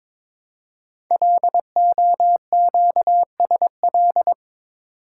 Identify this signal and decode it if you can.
Morse code, heard as a keyed tone, about 23 wpm, 700 Hz. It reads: LOQSL
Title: Code Groups - 5 Letters - 22wpm